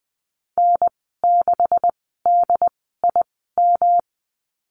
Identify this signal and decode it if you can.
Morse code, heard as a keyed tone, about 20 wpm, spaced as for 15 wpm, 700 Hz. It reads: N6DIM
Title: 167 Lesson 27g - Number 7 - Call Signs 20wpm